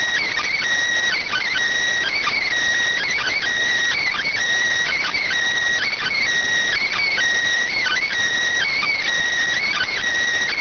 Emergency Alert System (SAME) tones (header bursts), heard as harsh buzzing bursts. It is heard from 1.1 to 10.0 s.
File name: findings / CT 205 ax mips, iDose 3 0006.gen-28 rx